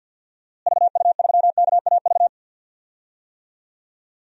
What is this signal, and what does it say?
Morse code, heard as a keyed tone, about 52 wpm, 700 Hz. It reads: VU3KAV
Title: DXCC-100 Call Signs - 6 Characters - Repeated 3x - 50wpm